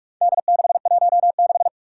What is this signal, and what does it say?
Morse code, heard as a keyed tone, about 46 wpm, 700 Hz. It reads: D616
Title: Four Challenging Letter-Number Combinations 45wpm